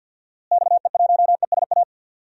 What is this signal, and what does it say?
Morse code, heard as a keyed tone, about 52 wpm, 700 Hz. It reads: XE1ESA
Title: DXCC-100 Call Signs - 6 Characters - Repeated 2x - 50wpm